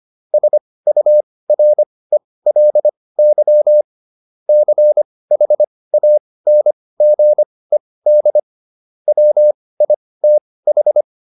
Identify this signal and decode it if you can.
Morse code, heard as a keyed tone, about 25 wpm, spaced as for 18 wpm, 600 Hz. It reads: SURELY CHANGED WITH